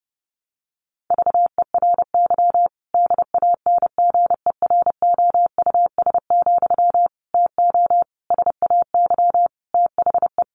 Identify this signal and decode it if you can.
Morse code, heard as a keyed tone, about 30 wpm, 700 Hz. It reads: VERY DANGEROUS, TO SAY THE
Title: Sentences from Top 1200 Words 30wpm